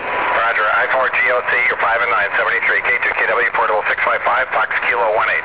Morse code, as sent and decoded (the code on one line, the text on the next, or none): none